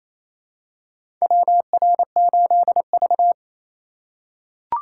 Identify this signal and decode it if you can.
Morse code, heard as a keyed tone, about 28 wpm, 700 Hz. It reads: WR8V
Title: USA Call Signs - 4 Characters - 4x Character Spacing - 28wpm